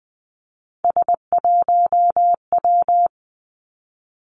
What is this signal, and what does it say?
Morse code, heard as a keyed tone, about 20 wpm, 700 Hz. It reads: S1W
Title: Three Challenging Letter-Number Combinations 20wpm